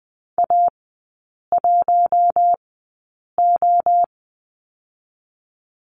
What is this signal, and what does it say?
Morse code, heard as a keyed tone, about 20 wpm, spaced as for 8 wpm, 700 Hz. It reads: A1O
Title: 067 Lesson 12e - Letter d - Three Characters 20wpm